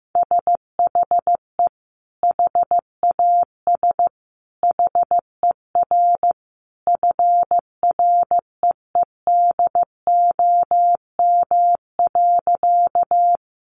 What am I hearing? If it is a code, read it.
Morse code, heard as a keyed tone, about 15 wpm, 700 Hz. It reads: SHE HAS HER FREEDOM.
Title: Sentences from Top 800 Words 15wpm